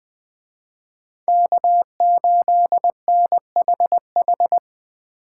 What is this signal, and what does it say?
Morse code, heard as a keyed tone, about 20 wpm, 700 Hz. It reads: K8NHH